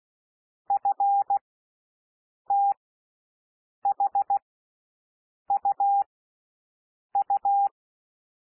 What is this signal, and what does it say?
Morse code, heard as a keyed tone, about 16 wpm, spaced as for 6 wpm, 800 Hz. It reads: FTHUU
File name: cwmp3 / L1 SOUND LETTERS 16 wpm